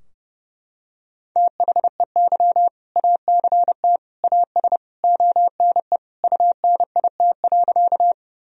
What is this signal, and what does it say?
Morse code, heard as a keyed tone, about 30 wpm, 700 Hz. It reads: THEY ACT AS ONE UNIT.
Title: Sentences from Top 1000 Words - Review Part 2 - 30wpm